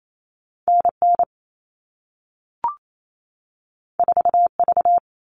Morse code, heard as a keyed tone, about 28 wpm, 700 Hz. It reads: NN 4V
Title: All Cut Numbers - 2-Digit 28wpm